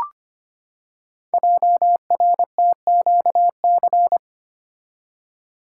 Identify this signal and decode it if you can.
Morse code, heard as a keyed tone, about 25 wpm, 700 Hz. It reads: JRTQC
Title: Code Groups - 5 Letters - 25wpm